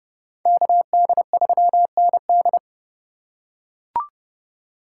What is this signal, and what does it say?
Morse code, heard as a keyed tone, about 27 wpm, 700 Hz. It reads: KD3ND